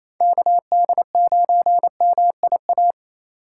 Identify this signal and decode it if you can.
Morse code, heard as a keyed tone, about 28 wpm, 700 Hz. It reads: KD9MIA
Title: USA Call Signs - 6 Characters - 4x Character Spacing - 28wpm